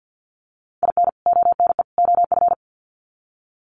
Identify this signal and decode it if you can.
Morse code, heard as a keyed tone, about 50 wpm, 700 Hz. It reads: IN ONE OF
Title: Sets of 3 Words 50wpm